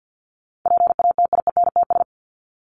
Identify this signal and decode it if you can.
Morse code, heard as a keyed tone, about 50 wpm, 700 Hz. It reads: PATIENTS
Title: ES:S Suffix - Top 100 Words with Base Words 50wpm